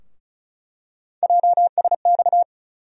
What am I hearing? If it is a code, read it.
Morse code, heard as a keyed tone, about 36 wpm, 700 Hz. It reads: JSX
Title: USA Call Sign Postfixes - 3 Letters 35wpm